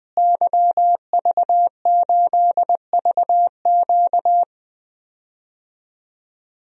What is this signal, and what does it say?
Morse code, heard as a keyed tone, about 20 wpm, 700 Hz. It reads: YV8VQ
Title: DXCC-100 Call Signs - 5 Characters - Repeated 2x - 20wpm